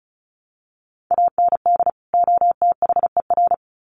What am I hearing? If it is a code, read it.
Morse code, heard as a keyed tone, about 34 wpm, 700 Hz. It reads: AND OTHER
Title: Sets of 2 Words 35wpm